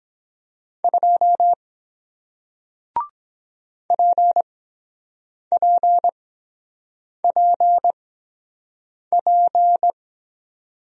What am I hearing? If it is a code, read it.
Morse code, heard as a keyed tone, about 23 wpm, spaced as for 7 wpm, 700 Hz. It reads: 2 PPPP